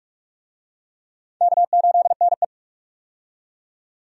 Morse code, heard as a keyed tone, about 46 wpm, 700 Hz. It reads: K8NE
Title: USA Call Signs - 4 Characters - 45wpm